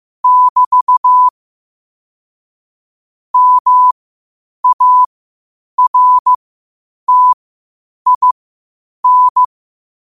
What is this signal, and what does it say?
Morse code, heard as a keyed tone, about 15 wpm, spaced as for 8 wpm, 1000 Hz. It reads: <BT> MARTIN
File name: qotdfiles / 8wpm